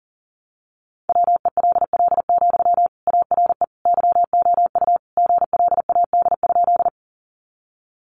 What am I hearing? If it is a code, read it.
Morse code, heard as a keyed tone, about 40 wpm, 700 Hz. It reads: WELL, ARE YOU GLAD?
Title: Sentences from Top 800 Words 40wpm